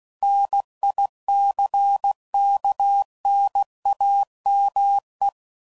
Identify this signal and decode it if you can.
Morse code, heard as a keyed tone, about 16 wpm, 800 Hz. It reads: NICKNAME
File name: CW Code Groups @15WPM / Technical Subjects / OR Gate